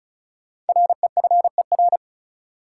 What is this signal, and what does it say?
Morse code, heard as a keyed tone, about 34 wpm, 700 Hz. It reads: REFER
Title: Sentences from Top 1000 Words - New Word Drill - 35wpm